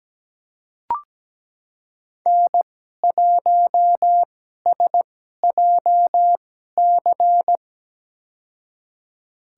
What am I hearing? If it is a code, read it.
Morse code, heard as a keyed tone, about 17 wpm, spaced as for 12 wpm, 700 Hz. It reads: N1SJC
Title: USA Call Signs - 5 Characters - 2x Character Spacing - 17wpm